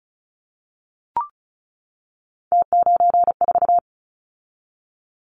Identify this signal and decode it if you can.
Morse code, heard as a keyed tone, about 36 wpm, 700 Hz. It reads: T94
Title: Callsign Prefixes for Top 100 DXCC - 35wpm